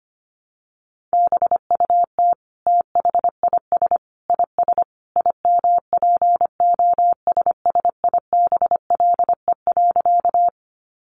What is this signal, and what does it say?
Morse code, heard as a keyed tone, about 25 wpm, 700 Hz. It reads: BUT THIS IS IMPOSSIBLE.
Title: Sentences from Top 1100 Words 25wpm